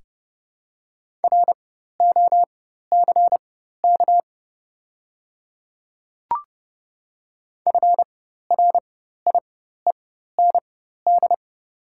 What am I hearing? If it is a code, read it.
Morse code, heard as a keyed tone, about 30 wpm, spaced as for 14 wpm, 690 Hz. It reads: ROCK FRIEND
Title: 4x Character Spacing - Top 500 Words - 30wpm